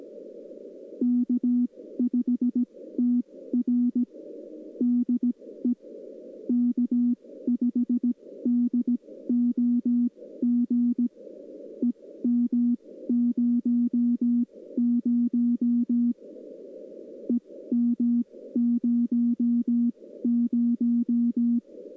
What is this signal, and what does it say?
Morse code, heard as a keyed tone, about 17 wpm, spaced as for 14 wpm, 240 Hz. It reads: K5TR DE K5DOG EM00 EM00
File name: contests / 2019-june=vhf-k5tr- -0101 0-2